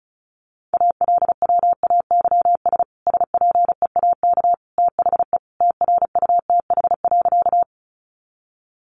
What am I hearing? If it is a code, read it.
Morse code, heard as a keyed tone, about 34 wpm, 700 Hz. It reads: ALWAYS SPEAK THE TRUTH.
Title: Sentences from Top 1100 Words 35wpm